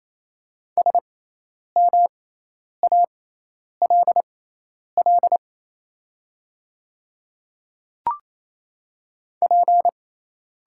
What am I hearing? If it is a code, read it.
Morse code, heard as a keyed tone, about 28 wpm, spaced as for 10 wpm, 700 Hz. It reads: SMALL P
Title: 6x Character Spacing - Top 200 Words - 28wpm